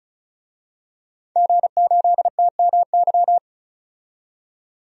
Morse code, heard as a keyed tone, about 34 wpm, 700 Hz. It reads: G8TMY
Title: DXCC-100 Call Signs - 5 Characters - 35wpm